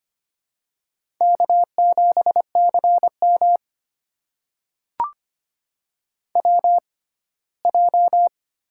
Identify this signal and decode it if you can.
Morse code, heard as a keyed tone, about 25 wpm, spaced as for 17 wpm, 700 Hz. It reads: K7CM WJ
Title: USA Call Signs - 4 Characters - 6x Character Spacing - 25wpm